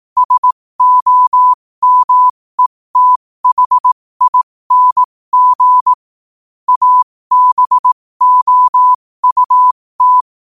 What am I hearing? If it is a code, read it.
Morse code, heard as a keyed tone, about 18 wpm, spaced as for 15 wpm, 1000 Hz. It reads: SOMETHING ABOUT
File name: qotdfiles / 15wpm